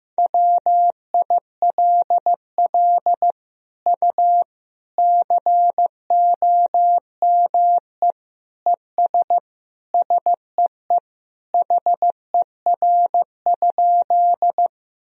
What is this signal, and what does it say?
Morse code, heard as a keyed tone, about 16 wpm, 700 Hz. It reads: WILL U COME ES SEE HER?
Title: Sentences from Top 100 Words and Abbreviations 15wpm